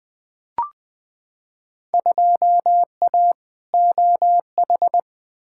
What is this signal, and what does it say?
Morse code, heard as a keyed tone, about 20 wpm, 700 Hz. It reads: 2A OH
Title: Field Day Exchanges - 20wpm